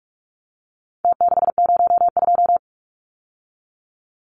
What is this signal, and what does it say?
Morse code, heard as a keyed tone, about 44 wpm, 700 Hz. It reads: T602
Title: Numbers - Contest Style - 4-Digit 45wpm